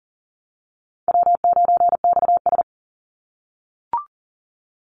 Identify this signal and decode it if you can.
Morse code, heard as a keyed tone, about 40 wpm, 700 Hz. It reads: W9XS
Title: Easy Call Signs - 40wpm